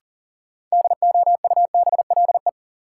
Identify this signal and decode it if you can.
Morse code, heard as a keyed tone, about 40 wpm, 700 Hz. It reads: DOUBLE
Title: Sentences from Top 1400 Words - New Word Drill - 40wpm